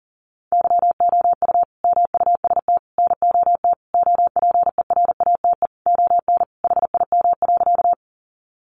Morse code, heard as a keyed tone, about 40 wpm, 700 Hz. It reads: YOU MUST NOT OPERATE ON HIM.
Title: Sentences from Top 1500 Words 40wpm